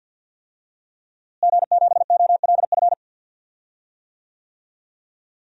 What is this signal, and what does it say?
Morse code, heard as a keyed tone, about 48 wpm, 700 Hz. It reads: G7OLF
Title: DXCC-100 Call Signs - 5 Characters - Repeated 2x - 50wpm